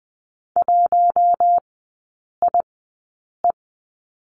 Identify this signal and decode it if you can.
Morse code, heard as a keyed tone, about 20 wpm, spaced as for 8 wpm, 700 Hz. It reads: 1IE
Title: 060 Lesson 11e - Letter h - Three Characters 20wpm